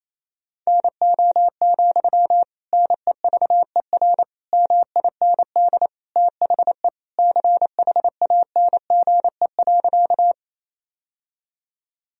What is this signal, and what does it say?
Morse code, heard as a keyed tone, about 28 wpm, 700 Hz. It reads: NO, NEVER MIND THE CHANGE.
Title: Sentences from Top 1000 Words - Review Part 1 - 28wpm